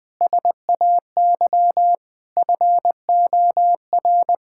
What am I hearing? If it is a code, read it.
Morse code, heard as a keyed tone, about 19 wpm, 700 Hz. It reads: SAY FOR
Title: Sentences from Top 1600 Words 20wpm